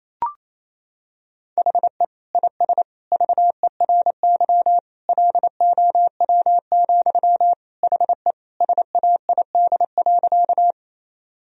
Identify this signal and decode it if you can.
Morse code, heard as a keyed tone, about 28 wpm, 700 Hz. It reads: HE IS VERY LOW, HE SAID.
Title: Sentences from Top 1000 Words - Review Part 1 - 28wpm